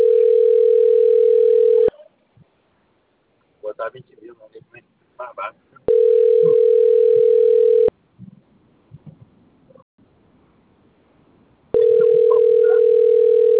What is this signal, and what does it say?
Ringback tone (North America), 440 + 480 Hz, with speech.
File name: rg-1600-unknown- - - .28001